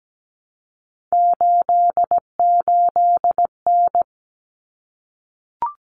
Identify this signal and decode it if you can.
Morse code, heard as a keyed tone, about 17 wpm, 700 Hz. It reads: 88N
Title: Common Cut Numbers - 3-Digit 17wpm